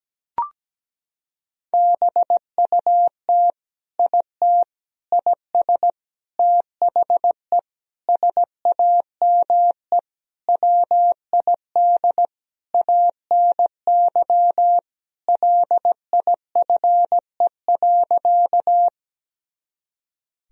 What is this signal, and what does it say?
Morse code, heard as a keyed tone, about 17 wpm, 700 Hz. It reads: BUT IT IS THE SAME WID ANY LIFE.
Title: Sentences from Top 300 Words and Abbreviations 17wpm